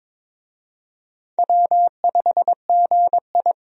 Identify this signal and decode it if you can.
Morse code, heard as a keyed tone, about 22 wpm, 700 Hz. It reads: W5GI
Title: USA Call Signs - 4 Characters - 22wpm